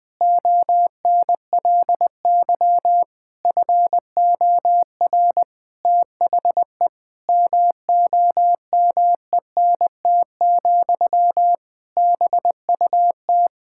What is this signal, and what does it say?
Morse code, heard as a keyed tone, about 20 wpm, 700 Hz. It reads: ONLY FOR THE MOMENT, BUT
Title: Sentences from Top 1000 Words - Review Part 6 - 20wpm